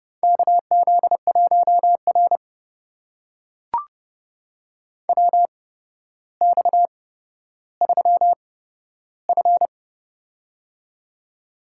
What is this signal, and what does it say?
Morse code, heard as a keyed tone, about 30 wpm, spaced as for 13 wpm, 700 Hz. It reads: KZ1R WX3F
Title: USA Call Signs - 4 Characters - 8x Character Spacing - 30wpm